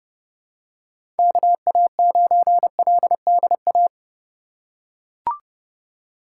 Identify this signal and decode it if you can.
Morse code, heard as a keyed tone, about 30 wpm, 700 Hz. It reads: KA9LDA
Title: USA Call Signs - 6 Characters - 3x Character Spacing - 30wpm